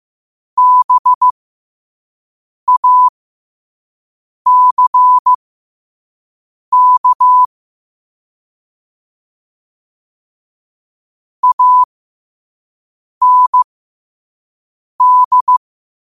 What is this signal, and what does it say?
Morse code, heard as a keyed tone, about 15 wpm, spaced as for 5 wpm, 1000 Hz. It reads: BACK AND
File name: qotdfiles / 5wpm